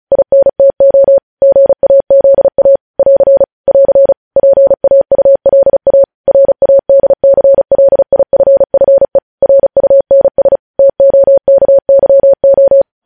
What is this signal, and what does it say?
Morse code, heard as a keyed tone, about 36 wpm, 550 Hz. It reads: INTO GAZA <AR> <AR> PAULA RADCLIFFE RUNS TOKYO
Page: News Headlines in Morse Code at 35 WPM